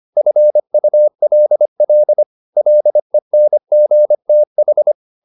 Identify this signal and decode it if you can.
Morse code, heard as a keyed tone, about 25 wpm, 600 Hz. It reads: FULL LENGTH
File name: the old man and the sea part 1 25wpm